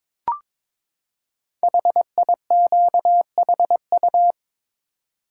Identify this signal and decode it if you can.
Morse code, heard as a keyed tone, about 22 wpm, 700 Hz. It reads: HIQHU
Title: Code Groups - 5 Letters - 22wpm